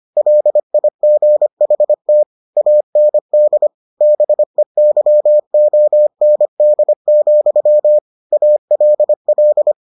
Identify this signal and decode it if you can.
Morse code, heard as a keyed tone, about 25 wpm, 600 Hz. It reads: LIGHT AND BEYOND, ALL